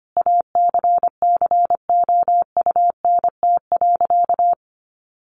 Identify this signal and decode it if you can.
Morse code, heard as a keyed tone, about 25 wpm, 700 Hz. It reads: ACCOUNT.
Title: Sentences from Top 900 Words 25wpm